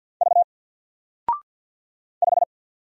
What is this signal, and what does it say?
Morse code, heard as a keyed tone, about 49 wpm, 700 Hz. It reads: V 5
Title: Single Letter-Number Alt 50wpm